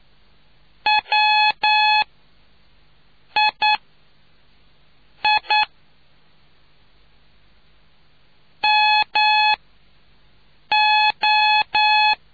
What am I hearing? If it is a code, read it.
Morse code, heard as a keyed tone, about 9 wpm, spaced as for 5 wpm, 850 Hz. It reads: WII MO